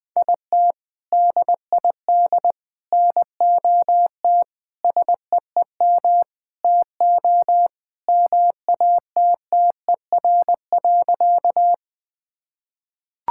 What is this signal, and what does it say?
Morse code, heard as a keyed tone, about 20 wpm, 690 Hz. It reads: IT DID NOT SEEM TO MATTER.
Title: Sentences with 4-Word Phrases - Spoken Beginning - 20wpm